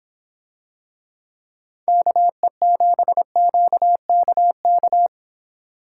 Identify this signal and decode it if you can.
Morse code, heard as a keyed tone, about 25 wpm, 700 Hz. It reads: KE7QKK